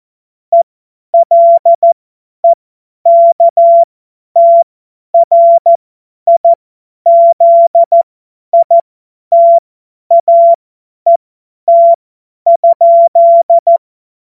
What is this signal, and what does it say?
Morse code, heard as a keyed tone, about 14 wpm, spaced as for 10 wpm, 680 Hz. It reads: ELEKTRIZITAET?
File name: Morsekurs / Witze / tz 08. November 2022 50 F70